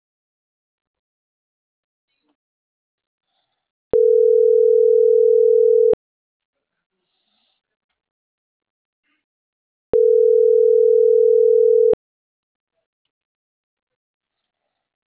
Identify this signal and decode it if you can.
Ringback tone (North America), 440 + 480 Hz.